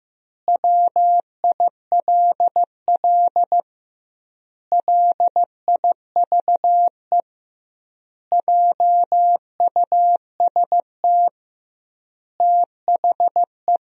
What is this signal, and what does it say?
Morse code, heard as a keyed tone, about 15 wpm, 700 Hz. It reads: WILL LIVE JUST THE
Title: 2x Word-Spacing - Sentences from Top 300 Words - 15wpm